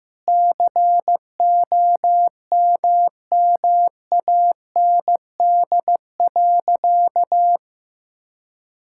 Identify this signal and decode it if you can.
Morse code, heard as a keyed tone, about 15 wpm, 700 Hz. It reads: COMMAND.
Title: Sentences from Top 1600 Words 15wpm